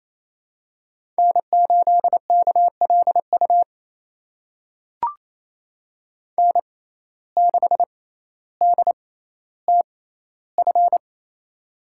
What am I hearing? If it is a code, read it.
Morse code, heard as a keyed tone, about 28 wpm, spaced as for 14 wpm, 700 Hz. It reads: N8KLU N6DTF
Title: USA Call Signs - 5 Characters - 6x Character Spacing - 28wpm